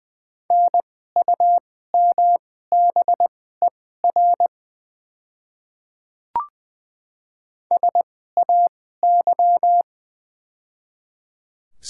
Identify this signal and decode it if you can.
Morse code, heard as a keyed tone, about 21 wpm, spaced as for 15 wpm, 700 Hz. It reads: NUMBER SAY
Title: 2x Character Spacing - Top 200 Words - 20wpm